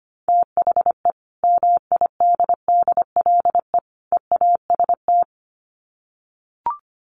Morse code, heard as a keyed tone, about 25 wpm, 700 Hz. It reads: THE MIDDLE EAST
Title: Sets of 3 Words 25wpm